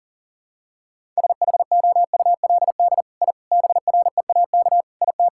Morse code, heard as a keyed tone, about 40 wpm, 700 Hz. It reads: SHOULD I BREAK IT
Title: Sentences from Top 1000 Words - Review Part 6 - 40wpm